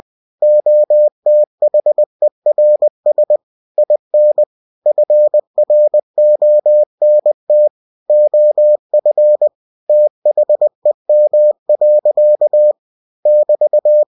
Morse code, heard as a keyed tone, about 20 wpm, 600 Hz. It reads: OTHERS IN FRONT OF THEM. <BT>
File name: winnie-20wpm